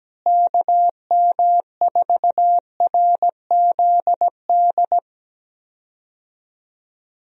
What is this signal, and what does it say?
Morse code, heard as a keyed tone, about 17 wpm, 700 Hz. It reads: KM4RZD
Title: USA Call Signs - 6 Characters - Repeated 2x - 17wpm